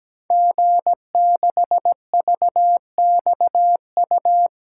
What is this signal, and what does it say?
Morse code, heard as a keyed tone, about 17 wpm, 700 Hz. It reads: G6VXU
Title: Code Groups - 5 Letter:Numbers - 17wpm